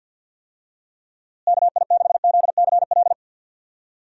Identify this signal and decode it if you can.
Morse code, heard as a keyed tone, about 51 wpm, 700 Hz. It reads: KI6ZCL
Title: USA Call Signs - 6 Characters - Repeated 2x - 50wpm